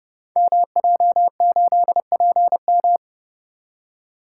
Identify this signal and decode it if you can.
Morse code, heard as a keyed tone, about 28 wpm, 700 Hz. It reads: MJ8PM
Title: Code Groups - 5 Letter:Numbers - 30wpm